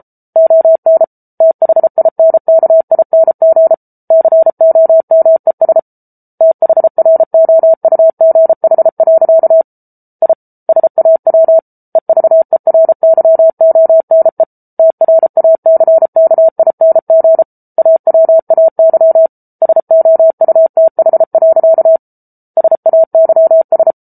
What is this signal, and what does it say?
Morse code, heard as a keyed tone, about 34 wpm, 650 Hz. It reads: ON THINKING COMES THROUGH. I SAW EVERYONE TRACKING AWAY SOUTH. SAYS